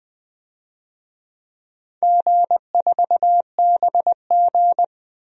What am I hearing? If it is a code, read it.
Morse code, heard as a keyed tone, about 20 wpm, 700 Hz. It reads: G4BG
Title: DXCC-100 Call Signs - 4 Characters - 20wpm